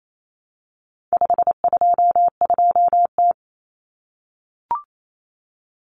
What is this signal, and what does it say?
Morse code, heard as a keyed tone, about 28 wpm, 700 Hz. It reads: 522T
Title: Common Cut Numbers - 4-Digit 28wpm